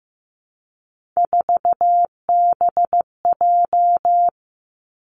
Morse code, heard as a keyed tone, about 15 wpm, 700 Hz. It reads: 4BJ
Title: Three Challenging Letter-Number Combinations 15wpm